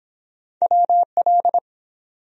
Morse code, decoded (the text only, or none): WL